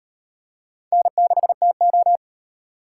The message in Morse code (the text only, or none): N6TO